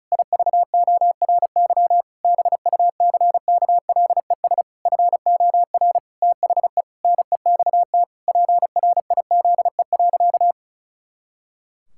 Morse code, heard as a keyed tone, about 34 wpm, 700 Hz. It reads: IVORY BUCKLES FOR THE NEXT PRIZE.
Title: Pangram Challenge v2 - 35wpm